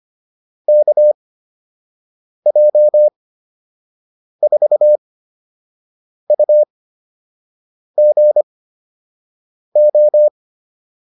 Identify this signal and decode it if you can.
Morse code, heard as a keyed tone, about 25 wpm, spaced as for 6 wpm, 600 Hz. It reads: KJ4UGO